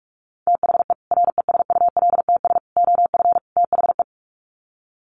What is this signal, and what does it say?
Morse code, heard as a keyed tone, about 44 wpm, 700 Hz. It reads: THE RESULTS OF THE
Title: Sets of 4 Words 45wpm